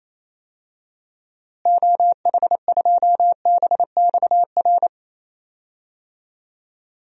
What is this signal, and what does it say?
Morse code, heard as a keyed tone, about 27 wpm, 700 Hz. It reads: OH2BXR
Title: DXCC-100 Call Signs - 6 Characters - Repeated 2x - 28wpm